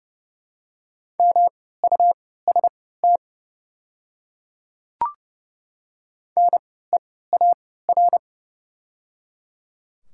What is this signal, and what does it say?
Morse code, heard as a keyed tone, about 30 wpm, spaced as for 17 wpm, 700 Hz. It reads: MUST NEAR
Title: 3x Character Spacing - Top 200 Words - 30wpm